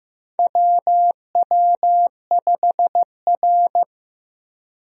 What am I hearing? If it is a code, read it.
Morse code, heard as a keyed tone, about 15 wpm, 700 Hz. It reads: WW5R
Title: USA Call Signs - 4 Characters - Repeated 2x - 15wpm